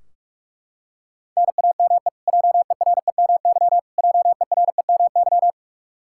Morse code, heard as a keyed tone, about 42 wpm, 700 Hz. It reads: NAME JEREMY JEREMY
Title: Practice QSO - Exchange 1 - Beginning - 45wpm